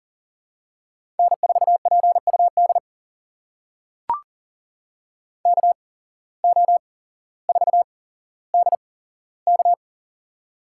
Morse code, heard as a keyed tone, about 40 wpm, spaced as for 17 wpm, 700 Hz. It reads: N4PUD KO4DK
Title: USA Call Signs - 5 Characters - 8x Character Spacing - 40wpm